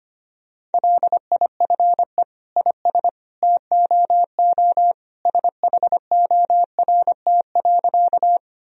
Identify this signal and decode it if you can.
Morse code, heard as a keyed tone, about 25 wpm, 700 Hz. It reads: LIFE IS TOO SHORT.
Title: Sentences from Top 1000 Words - Review Part 2 - 25wpm